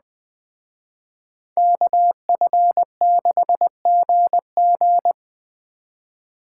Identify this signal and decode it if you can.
Morse code, heard as a keyed tone, about 20 wpm, 700 Hz. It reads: KF6GG
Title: Speed Racing - USA Call Signs 17wpm